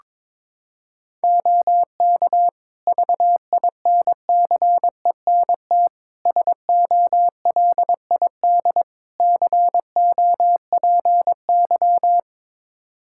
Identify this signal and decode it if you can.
Morse code, heard as a keyed tone, about 22 wpm, 700 Hz. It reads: OK VINCENT SOLID COPY
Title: Practice QSO - Exchange 3 - More Info - 22wpm